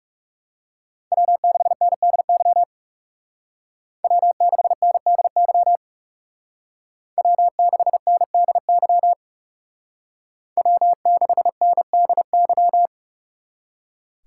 Morse code, heard as a keyed tone, about 38 wpm, 700 Hz. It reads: W6NDY W6NDY W6NDY W6NDY